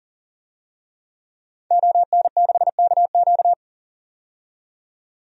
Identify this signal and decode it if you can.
Morse code, heard as a keyed tone, about 39 wpm, 700 Hz. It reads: ON6KQ
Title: DXCC-100 Call Signs - 5 Characters - 40wpm